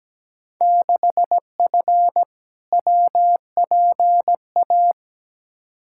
Morse code, heard as a keyed tone, about 17 wpm, 700 Hz. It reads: 6F WPA
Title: Field Day Exchanges Encore - 17wpm